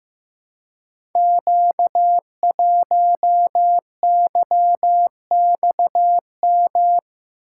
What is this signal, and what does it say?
Morse code, heard as a keyed tone, about 15 wpm, 700 Hz. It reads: Q1YXM
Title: Code Groups - 5 Letter:Numbers - 15wpm